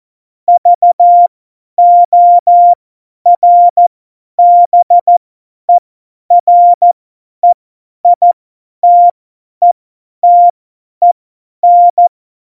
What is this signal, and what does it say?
Morse code, heard as a keyed tone, about 14 wpm, spaced as for 10 wpm, 700 Hz. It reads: VORBEREITETEN
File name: Morsekurs / Witze / Witz 17. November 2020, 50 F70